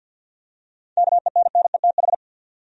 Morse code, heard as a keyed tone, about 50 wpm, 700 Hz. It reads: KENNETH
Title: USA Common Names - 50wpm